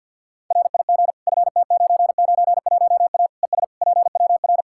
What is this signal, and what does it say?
Morse code, heard as a keyed tone, about 51 wpm, 700 Hz. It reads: RIG FT991A ES PWR
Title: K1USN SST QSO Practice 50wpm